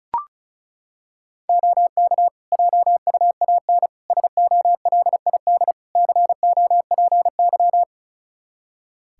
Morse code, heard as a keyed tone, about 34 wpm, 700 Hz. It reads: OK JUAN SOLID COPY
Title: Practice QSO - Exchange 3 - More Info - 35wpm